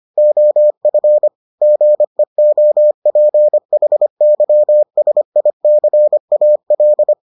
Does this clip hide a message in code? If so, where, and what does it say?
Yes: Morse code from 0.2 to 7.1 s, heard as a keyed tone, about 25 wpm, 600 Hz. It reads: OF GEOPHYSICAL